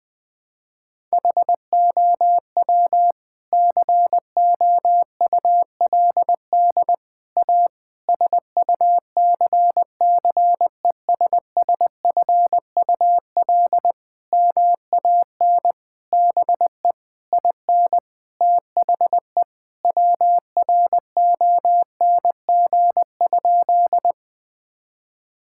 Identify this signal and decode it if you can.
Morse code, heard as a keyed tone, about 20 wpm, 700 Hz. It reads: HOW COULD A SUCCESSFUL MAN BE IN THE WRONG?
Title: Sentences from Top 1000 Words - Review Part 6 - 20wpm